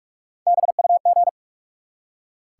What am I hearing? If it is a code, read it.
Morse code, heard as a keyed tone, about 44 wpm, 700 Hz. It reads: BUG
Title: Three Letter Words - 45wpm